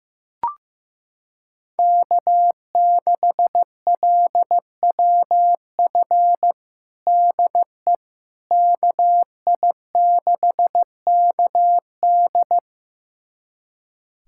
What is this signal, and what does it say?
Morse code, heard as a keyed tone, about 15 wpm, 700 Hz. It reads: K6LWF DE KI6KD
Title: USA Callsign DE Callsign - 5 Characters - 15wpm